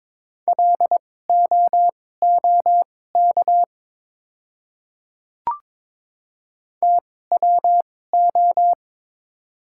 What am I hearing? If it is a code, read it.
Morse code, heard as a keyed tone, about 22 wpm, spaced as for 16 wpm, 700 Hz. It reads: LOOK TWO
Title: 2x Character Spacing - Top 100 Words - 22wpm